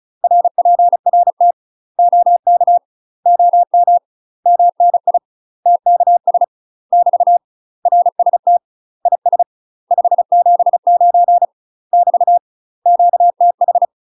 Morse code, heard as a keyed tone, about 34 wpm, 700 Hz. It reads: RPRT OK OM MNI TKS <BT> RST IS 579 <BT> QTH
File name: UK MASTER QSOs TRAINING / 35WPM / UK MASTER QSOs 0008